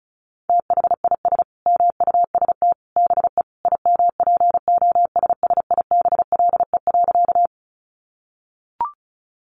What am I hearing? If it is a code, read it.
Morse code, heard as a keyed tone, about 34 wpm, 700 Hz. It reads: THIS MUST BE IMPOSSIBLE.
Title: Sentences from Top 1100 Words 35wpm